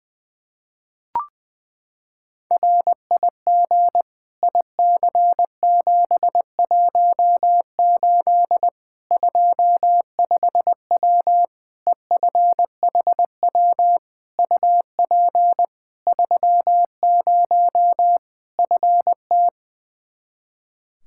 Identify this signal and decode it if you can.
Morse code, heard as a keyed tone, about 20 wpm, 700 Hz. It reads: RIG IC718 25W EFHW UP 30 FT
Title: Practice QSO - Exchange 2 - Info - 20wpm